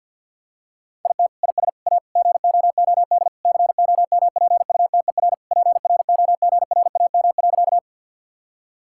Morse code, heard as a keyed tone, about 50 wpm, 700 Hz. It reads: IT IS A GOOD COMPUTER PROGRAM.
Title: Sentences from Top 1000 Words - Review Part 5 - 50wpm